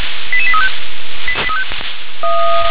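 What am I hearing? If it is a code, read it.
Five-tone selective call (ZVEI), heard as quick runs of five tones, tones of about 70 ms. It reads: 88835
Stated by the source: Five-tone ZVEI selective call received by SDR.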